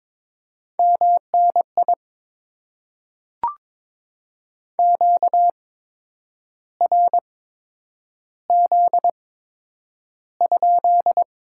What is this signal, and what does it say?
Morse code, heard as a keyed tone, about 23 wpm, spaced as for 9 wpm, 700 Hz. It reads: MNI QRZ?
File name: Top 90 QSO Elements - 8x Character Spacing - 22wpm